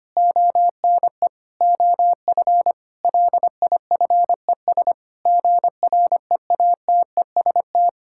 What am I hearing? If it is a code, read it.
Morse code, heard as a keyed tone, about 25 wpm, 700 Hz. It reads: ONE OF LIFES GREATEST
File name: practice / 25wpm / Difficult Sentences with Hints - 25wpm